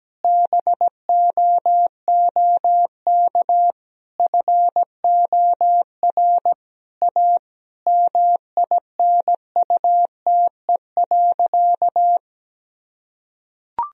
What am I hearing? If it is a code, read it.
Morse code, heard as a keyed tone, about 17 wpm, 700 Hz. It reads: BOOK FOR A MINUTE.
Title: Sentences from Top 1000 Words - Review Part 2 - 17wpm